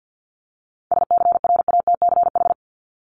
Morse code, heard as a keyed tone, about 50 wpm, 700 Hz. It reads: SCRATCH
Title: Top 1501-1750 Words 50wpm